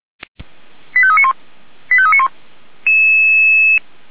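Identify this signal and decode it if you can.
Five-tone selective call (ZVEI), heard as quick runs of five tones, tones of about 70 ms. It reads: 85391 85391
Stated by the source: ZVEI five-tone selective call 89330 received by SDR.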